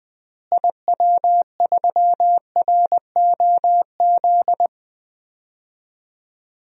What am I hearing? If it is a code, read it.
Morse code, heard as a keyed tone, about 21 wpm, 700 Hz. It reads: IW3ROZ
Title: DXCC-100 Call Signs - 6 Characters - Repeated 2x - 20wpm